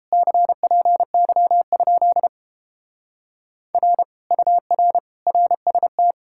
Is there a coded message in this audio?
Yes: Morse code from 0.1 to 6.1 s, heard as a keyed tone, about 29 wpm, 700 Hz. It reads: CPY? R UR RST